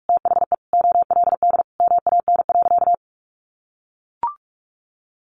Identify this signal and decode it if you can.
Morse code, heard as a keyed tone, about 44 wpm, 700 Hz. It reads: THE OLD MAN.
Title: Sentences from Top 500 Words 45wpm